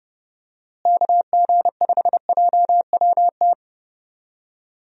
Morse code, heard as a keyed tone, about 31 wpm, 700 Hz. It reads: KG5JWT
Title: USA Call Signs - 6 Characters - Repeated 3x - 30wpm